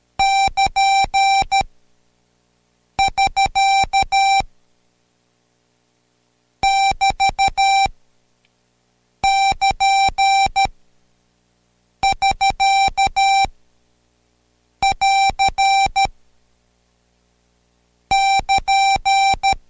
Morse code, heard as a keyed tone, about 13 wpm, 780 Hz. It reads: <KN> <SK> <BT> <KN> <SK> <AR> <KN>
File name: Code Quick / AudioDownld2 / Seg127-Code Quick Prosigns 2